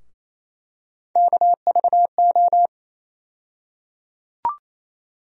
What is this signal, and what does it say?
Morse code, heard as a keyed tone, about 27 wpm, 700 Hz. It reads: KVO
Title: USA Call Sign Postfixes - 3 Letters 28wpm